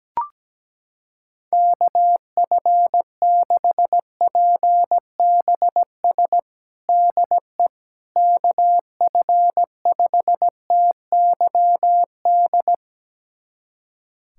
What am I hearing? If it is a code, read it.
Morse code, heard as a keyed tone, about 17 wpm, 700 Hz. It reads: KF6PBS DE KF5TYD
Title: USA Callsign DE Callsign - 6 Characters - 17wpm